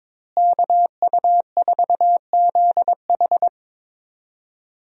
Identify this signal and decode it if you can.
Morse code, heard as a keyed tone, about 22 wpm, 700 Hz. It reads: KU4ZH
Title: USA Call Signs - 5 Characters - Repeated 3x - 22wpm